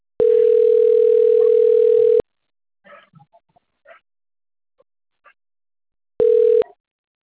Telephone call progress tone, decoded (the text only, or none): ringback tone